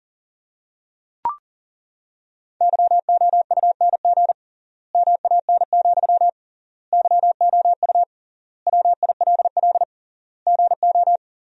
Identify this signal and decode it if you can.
Morse code, heard as a keyed tone, about 40 wpm, 700 Hz. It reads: YOUNG MAN, YOU WILL GO
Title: 3x Word-Spacing - Sentences from Top 400 Words - 40wpm